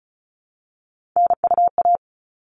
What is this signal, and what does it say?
Morse code, heard as a keyed tone, about 34 wpm, 700 Hz. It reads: NUA
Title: All Cut Numbers - 3-Digit 35wpm